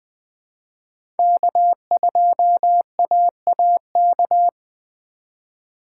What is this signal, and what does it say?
Morse code, heard as a keyed tone, about 20 wpm, 700 Hz. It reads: K2AAK
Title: USA Call Signs - 5 Characters - 6x Character Spacing - 20wpm